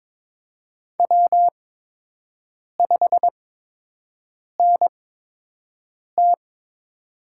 Morse code, heard as a keyed tone, about 22 wpm, spaced as for 6 wpm, 700 Hz. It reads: W5NT
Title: USA Call Signs - 4 Characters - 8x Character Spacing - 22wpm